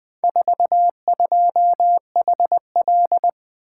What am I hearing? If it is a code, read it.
Morse code, heard as a keyed tone, about 20 wpm, 700 Hz. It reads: 42HL
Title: Four Letter-Number Combinations - 20wpm